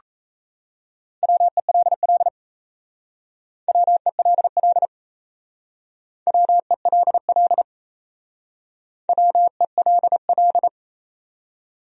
Morse code, heard as a keyed tone, about 35 wpm, 700 Hz. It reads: WELL WELL WELL WELL